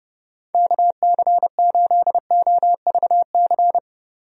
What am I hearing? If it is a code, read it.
Morse code, heard as a keyed tone, about 30 wpm, 700 Hz. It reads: KC8OVC
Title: USA Call Signs - 6 Characters - 30wpm